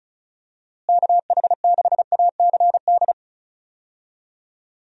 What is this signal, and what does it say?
Morse code, heard as a keyed tone, about 36 wpm, 700 Hz. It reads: KH6ACD
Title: USA Call Signs - 6 Characters - Repeated 3x - 35wpm